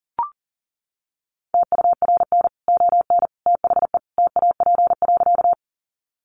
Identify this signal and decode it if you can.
Morse code, heard as a keyed tone, about 40 wpm, 700 Hz. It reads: TURN ON THE TAP.
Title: Sentences from Top 1400 Words 40wpm